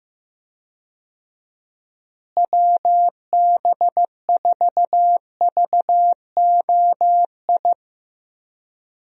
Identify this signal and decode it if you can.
Morse code, heard as a keyed tone, about 15 wpm, 700 Hz. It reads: WB4VOI
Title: USA Call Signs - 6 Characters - Repeated 3x - 15wpm